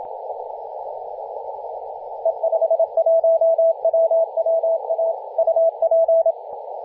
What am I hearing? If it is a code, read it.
Morse code, heard as a keyed tone, about 27 wpm, 660 Hz, with radio noise. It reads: E51WWA UP
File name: e51wwa-10-104mhz-cw